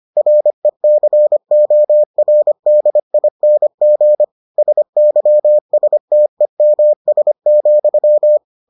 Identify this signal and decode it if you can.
Morse code, heard as a keyed tone, about 25 wpm, 600 Hz. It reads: RECORDING SYSTEMS,